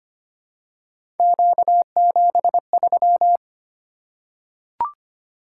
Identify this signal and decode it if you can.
Morse code, heard as a keyed tone, about 25 wpm, 700 Hz. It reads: Q73
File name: Three Letter-Number Combinations - 25wpm